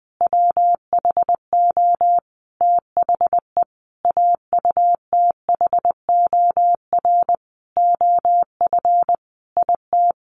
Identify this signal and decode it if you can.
Morse code, heard as a keyed tone, about 20 wpm, 700 Hz. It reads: WHO THE AUTHOR OF IT
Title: Sentences from Top 1200 Words 20wpm